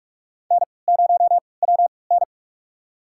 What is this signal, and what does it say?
Morse code, heard as a keyed tone, about 45 wpm, spaced as for 25 wpm, 700 Hz. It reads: N0WN
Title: USA Call Signs - 4 Characters - 3x Character Spacing - 45wpm